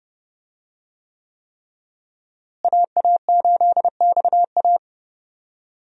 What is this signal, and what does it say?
Morse code, heard as a keyed tone, about 30 wpm, 700 Hz. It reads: AA8XA
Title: USA Call Signs - 5 Characters - Repeated 3x - 30wpm